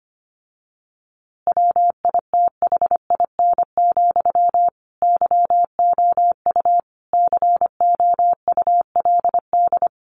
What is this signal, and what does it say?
Morse code, heard as a keyed tone, about 25 wpm, 700 Hz. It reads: WITHIN, YOU COULD
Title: A Sound Of Thunder 25wpm